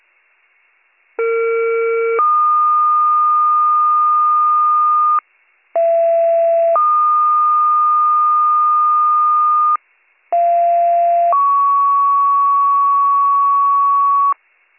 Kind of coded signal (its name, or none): two-tone page